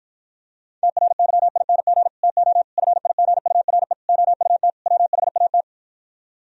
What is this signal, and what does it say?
Morse code, heard as a keyed tone, about 52 wpm, 700 Hz. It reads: TRYING TO FIGURE OUT WHAT